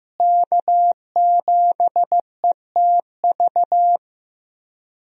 Morse code, heard as a keyed tone, about 15 wpm, 700 Hz. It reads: K7ETV
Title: USA Call Signs - 5 Characters - Repeated 3x - 15wpm